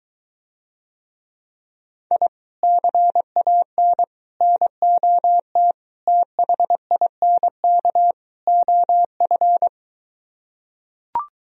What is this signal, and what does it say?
Morse code, heard as a keyed tone, about 23 wpm, 700 Hz. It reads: I CAN NOT THINK OF